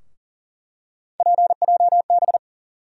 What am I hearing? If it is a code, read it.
Morse code, heard as a keyed tone, about 40 wpm, 700 Hz. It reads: PJB
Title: USA Call Sign Postfixes - 3 Letters 40wpm